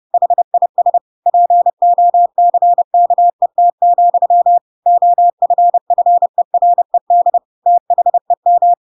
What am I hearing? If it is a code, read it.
Morse code, heard as a keyed tone, about 30 wpm, 700 Hz. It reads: HIS POCKET, OFFERED THEM
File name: Pinocchio / Pinocchio ch7 30 30 ik1hge